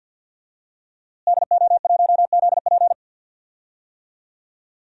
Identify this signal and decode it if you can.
Morse code, heard as a keyed tone, about 50 wpm, 700 Hz. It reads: DO1ZP
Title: DXCC-100 Call Signs - 5 Characters - Repeated 3x - 50wpm